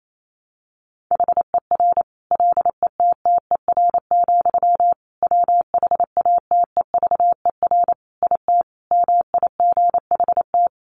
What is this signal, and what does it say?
Morse code, heard as a keyed tone, about 28 wpm, 700 Hz. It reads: HER LETTER, WHATEVER IT MIGHT
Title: Sentences from Top 1100 Words 28wpm